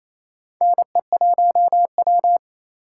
Morse code, heard as a keyed tone, about 28 wpm, 700 Hz. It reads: NE1W
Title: USA Call Signs - 4 Characters - 2x Character Spacing - 28wpm